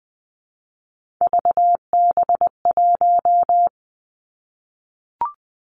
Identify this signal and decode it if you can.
Morse code, heard as a keyed tone, about 20 wpm, 700 Hz. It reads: VB1
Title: Three Challenging Letter-Number Combinations 20wpm